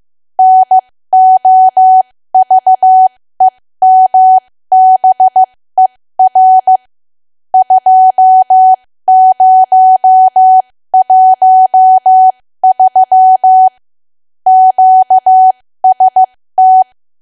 Morse code, heard as a keyed tone, about 15 wpm, spaced as for 13 wpm, 750 Hz. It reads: NOVEMBER 2013 QST
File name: cwa / W1AW-Level2 / 13WPM